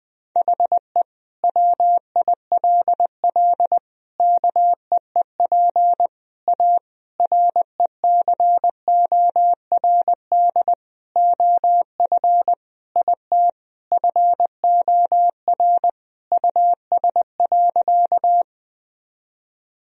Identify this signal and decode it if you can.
Morse code, heard as a keyed tone, about 20 wpm, 700 Hz. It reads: HE WILL KEEP A RECORD OF IT FOR US.
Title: Sentences from Top 1000 Words - Review Part 3 - 20wpm